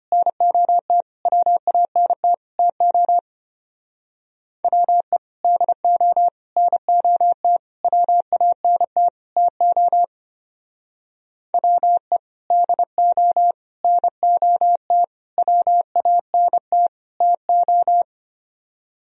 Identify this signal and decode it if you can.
Morse code, heard as a keyed tone, about 29 wpm, 700 Hz. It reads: NOT WANT TO WE DO NOT WANT TO WE DO NOT WANT TO
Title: Speed-Racing - Sets of 5 Words 25wpm